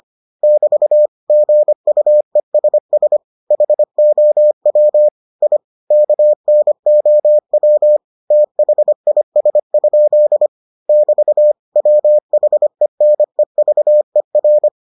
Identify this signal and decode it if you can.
Morse code, heard as a keyed tone, about 25 wpm, 600 Hz. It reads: <BT> GUESS HOW I KNOW THIS? <BT> WHENEVER